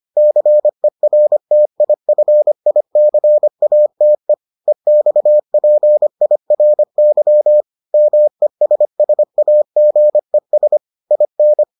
Morse code, heard as a keyed tone, about 25 wpm, 600 Hz. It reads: CERTIFICATE EXPIRY MESSAGES IN